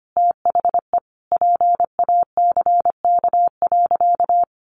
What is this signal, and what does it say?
Morse code, heard as a keyed tone, about 25 wpm, 700 Hz. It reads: THE PACK.
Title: Sentences from Top 1300 Words 25wpm